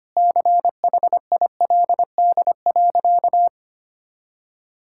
Morse code, heard as a keyed tone, about 25 wpm, 700 Hz. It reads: CHILD.
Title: Sentences from Top 1000 Words - Review Part 6 - 25wpm